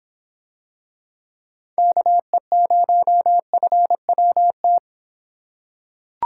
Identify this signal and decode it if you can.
Morse code, heard as a keyed tone, about 25 wpm, 700 Hz. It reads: KE0FWT